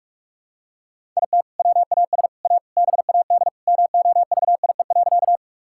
Morse code, heard as a keyed tone, about 42 wpm, 700 Hz. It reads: IT WAS A BAD MOVIE.
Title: Sentences from Top 1000 Words - Review Part 5 - 45wpm